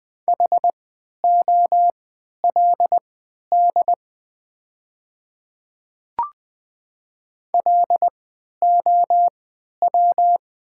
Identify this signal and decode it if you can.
Morse code, heard as a keyed tone, about 20 wpm, spaced as for 11 wpm, 690 Hz. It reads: HOLD LOW
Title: 3x Character Spacing - Top 500 Words - 20wpm